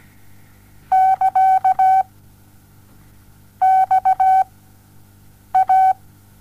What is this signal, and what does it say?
Morse code, heard as a keyed tone, about 17 wpm, 760 Hz. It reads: <KA> X A
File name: Onlinelehrgang / morsen / DJ4UF Morsen07